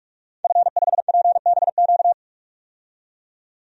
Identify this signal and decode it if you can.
Morse code, heard as a keyed tone, about 43 wpm, 700 Hz. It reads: F5PBQ
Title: DXCC-100 Call Signs - 5 Characters - 45wpm